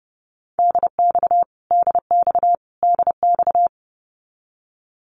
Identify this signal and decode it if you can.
Morse code, heard as a keyed tone, about 30 wpm, 700 Hz. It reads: DX DX DX
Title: Abbreviations Repeated 3x - 30wpm